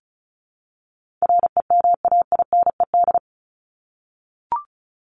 Morse code, heard as a keyed tone, about 34 wpm, 700 Hz. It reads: REMAINED
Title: ED Suffix - Top 100 Words with Base Words 35wpm